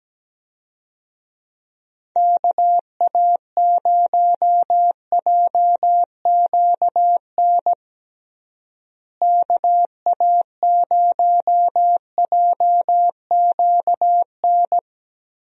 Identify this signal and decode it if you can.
Morse code, heard as a keyed tone, about 17 wpm, 700 Hz. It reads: KA0JQN KA0JQN
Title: USA Call Signs - 6 Characters - Repeated 3x - 17wpm